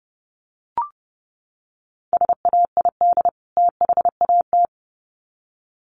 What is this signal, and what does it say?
Morse code, heard as a keyed tone, about 30 wpm, 700 Hz. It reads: SAID THAT
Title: Sets of 2 Words 30wpm